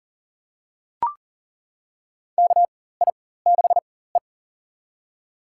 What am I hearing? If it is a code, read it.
Morse code, heard as a keyed tone, about 40 wpm, spaced as for 19 wpm, 700 Hz. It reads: KI6E
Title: USA Call Signs - 4 Characters - 4x Character Spacing - 40wpm